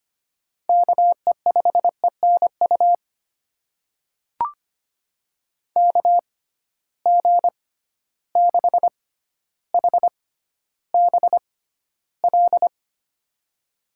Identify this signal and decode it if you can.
Morse code, heard as a keyed tone, about 25 wpm, spaced as for 13 wpm, 700 Hz. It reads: KE5ENU KG6HBL
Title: USA Call Signs - 6 Characters - 6x Character Spacing - 25wpm